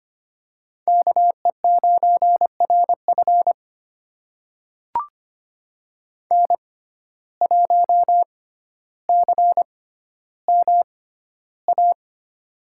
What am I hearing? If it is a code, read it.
Morse code, heard as a keyed tone, about 25 wpm, spaced as for 13 wpm, 700 Hz. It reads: KE9RF N1CMA
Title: USA Call Signs - 6x Character Spacing - 25wpm